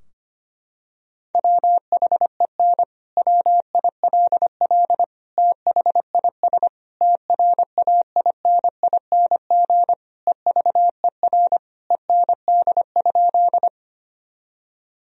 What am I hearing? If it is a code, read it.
Morse code, heard as a keyed tone, about 25 wpm, 700 Hz. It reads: WHEN WILL THIS TRAINING EVER END?
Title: Sentences from Top 1000 Words - Review Part 4 - 25wpm